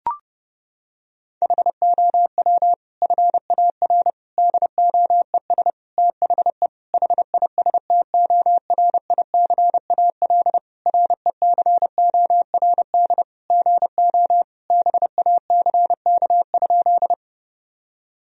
Morse code, heard as a keyed tone, about 32 wpm, 700 Hz. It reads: HOW FAR DOES THE HISTORICAL RECORD GO BACK?
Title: Sentences from Top 1000 Words - Review Part 4 - 30wpm